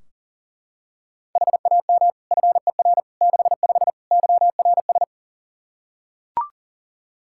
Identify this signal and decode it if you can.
Morse code, heard as a keyed tone, about 40 wpm, 700 Hz. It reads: HAM FER 65 YRS
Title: K1USN SST QSO Practice 40wpm